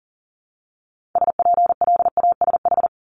Morse code, heard as a keyed tone, about 41 wpm, 700 Hz. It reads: SPLASH
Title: Multi-letter Phonemes 40wpm